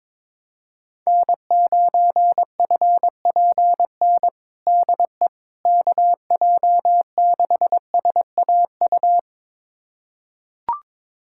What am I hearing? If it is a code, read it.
Morse code, heard as a keyed tone, about 22 wpm, 700 Hz. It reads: N9FPN DE KJ6SAU
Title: USA Callsign DE Callsign 22wpm